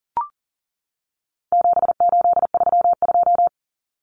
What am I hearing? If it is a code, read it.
Morse code, heard as a keyed tone, about 40 wpm, 700 Hz. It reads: 7832
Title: Common Cut Numbers - 4-Digit 40wpm